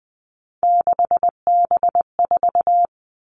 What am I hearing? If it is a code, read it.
Morse code, heard as a keyed tone, about 20 wpm, 700 Hz. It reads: 6B4
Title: All Cut Numbers - 3-Digit 20wpm